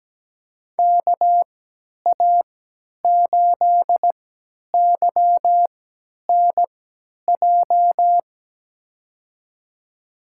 Morse code, heard as a keyed tone, about 17 wpm, spaced as for 10 wpm, 700 Hz. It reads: KA8YNJ
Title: USA Call Signs - 3x Character Spacing - 17wpm